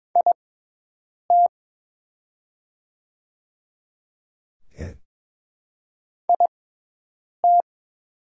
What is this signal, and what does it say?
Morse code, heard as a keyed tone, about 22 wpm, spaced as for 8 wpm, 700 Hz. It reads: IT IT
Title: 6x Character Spacing - Top 100 Words - 22wpm